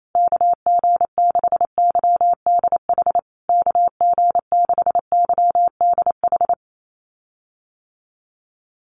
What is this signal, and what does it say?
Morse code, heard as a keyed tone, about 28 wpm, 700 Hz. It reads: KG6YDH KG6YDH
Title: USA Call Signs Repeated 3x 28wpm